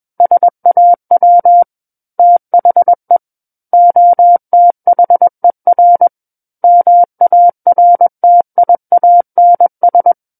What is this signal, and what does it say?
Morse code, heard as a keyed tone, about 21 wpm, 700 Hz. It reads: SAW THE OTHER MARTIANS
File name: Book One / 12-Chapter 12 - 21 WPM